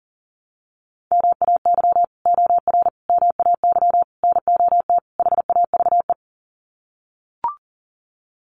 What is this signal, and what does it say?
Morse code, heard as a keyed tone, about 40 wpm, 700 Hz. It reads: MAY OR MAY NOT HAVE
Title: Sets of 5 Words Encore2 40wpm